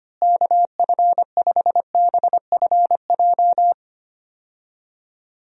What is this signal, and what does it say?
Morse code, heard as a keyed tone, about 25 wpm, 700 Hz. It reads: KF5BFJ
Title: USA Call Signs - 6 Characters - Repeated 3x - 25wpm